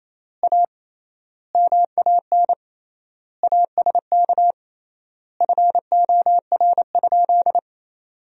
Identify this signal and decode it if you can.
Morse code, heard as a keyed tone, about 28 wpm, 700 Hz. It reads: A MAN ASK FOR?
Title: 3x Word-Spacing - Sentences from Top 200 Words - 28wpm